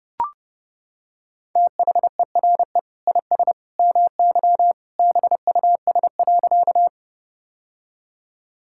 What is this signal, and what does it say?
Morse code, heard as a keyed tone, about 30 wpm, 700 Hz. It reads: THERE IS MY BUS.
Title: Sentences from Top 1000 Words - Review Part 6 - 30wpm